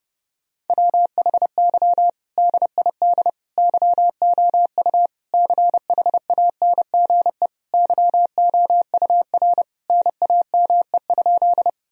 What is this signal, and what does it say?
Morse code, heard as a keyed tone, about 32 wpm, 700 Hz. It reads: WHY DID YOU CHANGE YOUR NAME?
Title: Sentences from Top 1000 Words - Review Part 1 - 30wpm